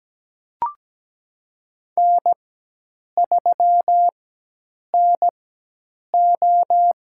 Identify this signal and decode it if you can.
Morse code, heard as a keyed tone, about 17 wpm, spaced as for 8 wpm, 700 Hz. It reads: N3NO
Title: USA Call Signs - 4 Characters - 4x Character Spacing - 17wpm